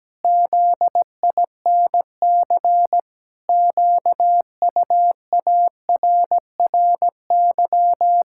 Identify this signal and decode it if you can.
Morse code, heard as a keyed tone, about 17 wpm, 700 Hz. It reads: ZINC QUARRY
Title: Pangram Challenge v2 - 17wpm